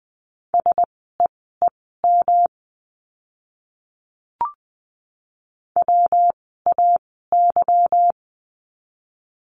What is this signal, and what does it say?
Morse code, heard as a keyed tone, about 20 wpm, spaced as for 15 wpm, 700 Hz. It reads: SEEM WAY
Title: 237 Lesson 38f - Letter q - Words 20wpm